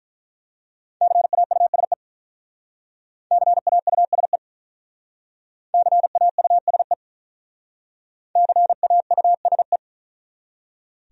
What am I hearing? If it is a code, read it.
Morse code, heard as a keyed tone, about 43 wpm, 700 Hz. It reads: CAUSE CAUSE CAUSE CAUSE
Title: Speed-Racing Top 300 Words 35wpm